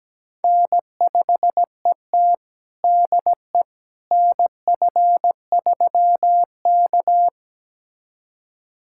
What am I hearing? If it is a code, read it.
Morse code, heard as a keyed tone, about 17 wpm, 700 Hz. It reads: N5ET DE NF3K
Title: USA Callsign DE Callsign - 4 Characters - 17wpm